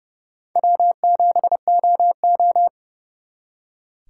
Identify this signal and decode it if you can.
Morse code, heard as a keyed tone, about 38 wpm, 700 Hz. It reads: W7OO